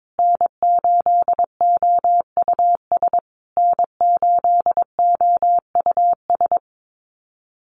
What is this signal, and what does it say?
Morse code, heard as a keyed tone, about 22 wpm, 700 Hz. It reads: N8OUS N8OUS
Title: USA Call Signs Repeated 3x 22wpm